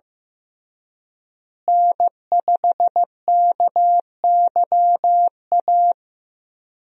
Morse code, heard as a keyed tone, about 14 wpm, 700 Hz. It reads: N5KYA
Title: USA Call Signs - 5 Characters - Repeated 2x - 15wpm